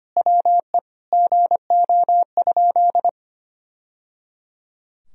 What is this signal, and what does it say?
Morse code, heard as a keyed tone, about 25 wpm, 700 Hz. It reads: WE GO?
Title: Sentences from Top 1000 Words - Review Part 2 - 25wpm